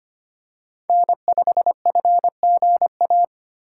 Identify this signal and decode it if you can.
Morse code, heard as a keyed tone, about 25 wpm, 700 Hz. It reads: N5FGA
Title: USA Call Signs - 5 Characters - 2x Character Spacing - 25wpm